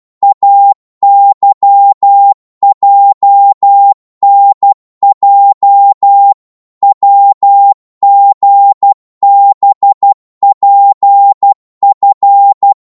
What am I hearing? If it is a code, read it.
Morse code, heard as a keyed tone, about 12 wpm, 800 Hz. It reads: AYJNJ WGBPF
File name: Morse / ltr-12wpm